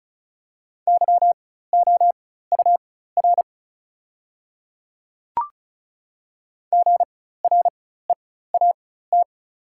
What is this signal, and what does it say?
Morse code, heard as a keyed tone, about 35 wpm, spaced as for 16 wpm, 700 Hz. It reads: YOUR GREAT
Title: 4x Character Spacing - Top 300 Words - 35wpm